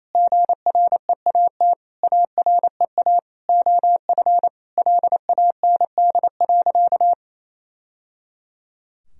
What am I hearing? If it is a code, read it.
Morse code, heard as a keyed tone, about 28 wpm, 700 Hz. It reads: GREAT AREA OF LAND.
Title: Sentences from Top 1000 Words - Review Part 2 - 28wpm